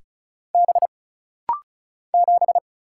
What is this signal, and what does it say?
Morse code, heard as a keyed tone, about 36 wpm, 700 Hz. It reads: B 7
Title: Single Letter-Number - Rapid-Fire 35wpm